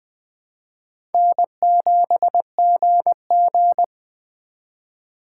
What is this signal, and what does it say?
Morse code, heard as a keyed tone, about 23 wpm, 700 Hz. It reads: N7GG
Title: Speed Racing - USA 1x2 Call Signs 20wpm